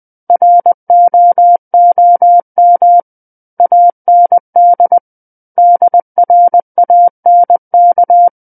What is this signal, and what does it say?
Morse code, heard as a keyed tone, about 20 wpm, 700 Hz. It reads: ROOM AND DRANK
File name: 20 WPM